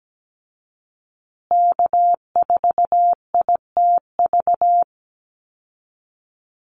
Morse code, heard as a keyed tone, about 17 wpm, 700 Hz. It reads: K4ITV
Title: Call Signs Repeated 3x - 17wpm